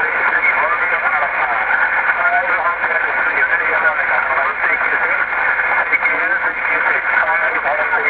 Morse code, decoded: none